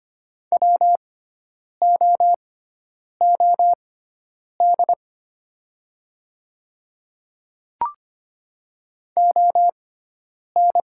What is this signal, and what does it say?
Morse code, heard as a keyed tone, about 25 wpm, spaced as for 9 wpm, 700 Hz. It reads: WOOD ON
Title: 6x Character Spacing - Top 500 Words - 25wpm